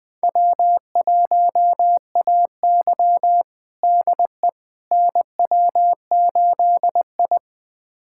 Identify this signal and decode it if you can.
Morse code, heard as a keyed tone, about 21 wpm, 700 Hz. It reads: W1AY DE NW8I
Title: USA Callsign DE Callsign - 4 Characters - 20wpm